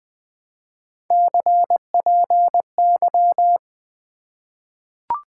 Morse code, heard as a keyed tone, about 20 wpm, 700 Hz. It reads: CPY
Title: Top 90 QSO Elements - 2x Character Spacing - 20wpm